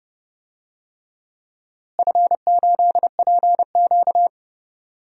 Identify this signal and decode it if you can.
Morse code, heard as a keyed tone, about 32 wpm, 700 Hz. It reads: F8PQ
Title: DXCC-100 Call Signs - 4 Characters - Repeated 3x - 30wpm